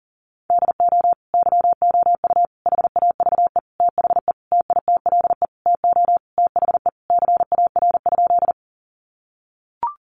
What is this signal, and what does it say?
Morse code, heard as a keyed tone, about 40 wpm, 700 Hz. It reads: DO YOU HAVE THE TITLE TO THE CAR?
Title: Sentences from Top 1200 Words 40wpm